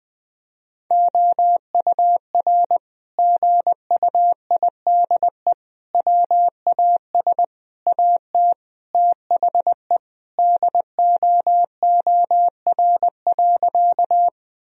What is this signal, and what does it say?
Morse code, heard as a keyed tone, about 20 wpm, 700 Hz. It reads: OUR GUIDE WAS AT THE DOOR.
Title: Sentences from Top 1000 Words - Review Part 3 - 20wpm